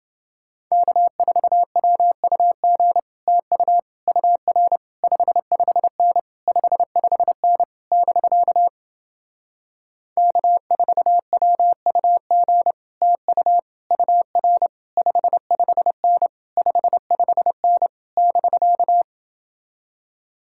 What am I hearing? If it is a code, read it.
Morse code, heard as a keyed tone, about 28 wpm, 700 Hz. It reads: K4WUG TU UR 55N 55N <BK> K4WUG TU UR 55N 55N <BK>